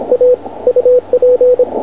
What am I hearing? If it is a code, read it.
Morse code, heard as a keyed tone, about 26 wpm, 500 Hz, with radio noise. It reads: A UP